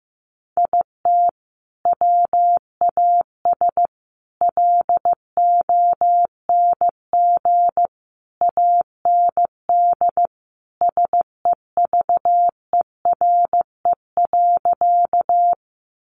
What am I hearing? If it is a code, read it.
Morse code, heard as a keyed tone, about 15 wpm, 700 Hz. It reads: IT WAS LONG AND SEVERE.
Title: Sentences from Top 1400 Words 15wpm